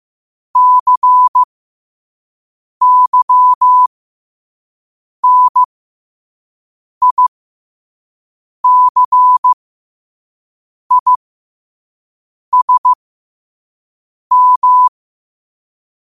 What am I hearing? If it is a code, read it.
Morse code, heard as a keyed tone, about 15 wpm, spaced as for 5 wpm, 1000 Hz. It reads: CYNICISM
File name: qotdfiles / 5wpm